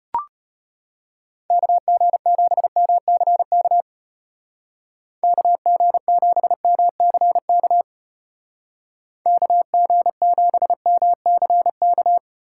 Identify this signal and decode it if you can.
Morse code, heard as a keyed tone, about 30 wpm, 700 Hz. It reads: KG7MCK KG7MCK KG7MCK
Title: Speed Racing - USA 2x3 Call Signs 25wpm